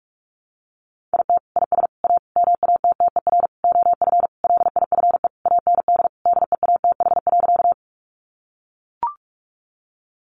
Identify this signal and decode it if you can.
Morse code, heard as a keyed tone, about 44 wpm, 700 Hz. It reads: IT IS A MATTER OF LIFE AND DEATH.
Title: Sentences from Top 1100 Words 45wpm